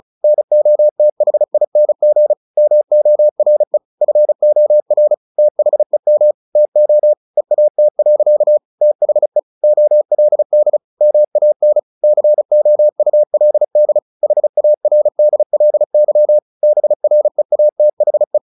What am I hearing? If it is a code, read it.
Morse code, heard as a keyed tone, about 34 wpm, 600 Hz. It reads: NOTHING MORE FOR THEM TO EAT. THE OLD MAN COULD HARDLY BREATHE
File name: the old man and the sea 2 35wpm